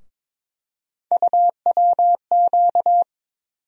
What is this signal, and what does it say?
Morse code, heard as a keyed tone, about 22 wpm, 700 Hz. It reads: UWQ
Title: USA Call Sign Postfixes - 3 Letters 22wpm